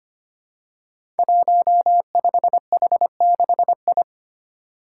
Morse code, heard as a keyed tone, about 25 wpm, 700 Hz. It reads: 15H6I
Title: Five Challenging Letter-Number Combinations 25wpm